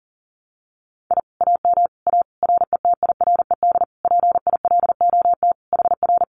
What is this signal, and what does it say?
Morse code, heard as a keyed tone, about 38 wpm, 700 Hz. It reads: I AM A RETIRED PILOT HR
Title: Common QSO Phrases Encore 40wpm